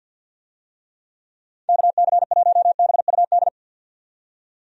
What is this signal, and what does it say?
Morse code, heard as a keyed tone, about 46 wpm, 700 Hz. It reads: KC1BUD
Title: USA Call Signs - 6 Characters - 50wpm